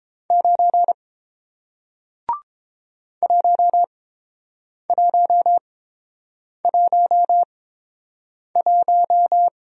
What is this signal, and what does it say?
Morse code, heard as a keyed tone, about 30 wpm, spaced as for 7 wpm, 700 Hz. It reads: 9 1111